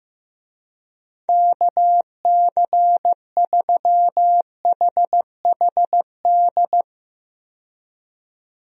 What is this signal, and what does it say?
Morse code, heard as a keyed tone, about 15 wpm, 700 Hz. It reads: KC3HHD
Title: USA Call Signs - 6 Characters - 15wpm